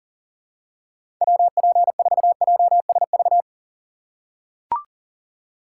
Morse code, heard as a keyed tone, about 40 wpm, 700 Hz. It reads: WP4JSV
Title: USA Call Signs - 6 Characters - 2x Character Spacing - 40wpm